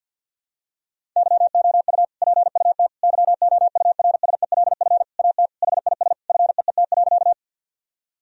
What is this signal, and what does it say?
Morse code, heard as a keyed tone, about 51 wpm, 700 Hz. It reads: YOU PUT YOURSELF AT HIS FEET.
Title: Sentences from Top 1000 Words - Review Part 1 - 50wpm